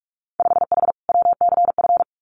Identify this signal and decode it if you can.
Morse code, heard as a keyed tone, about 44 wpm, 700 Hz. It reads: 5H WCF
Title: Winter Field Day Exchanges Encore 45wpm